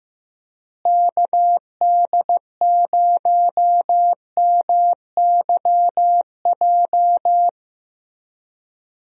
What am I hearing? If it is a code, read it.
Morse code, heard as a keyed tone, about 16 wpm, 700 Hz. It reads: KD0MYJ
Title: USA Call Signs - 6 Characters - Repeated 2x - 15wpm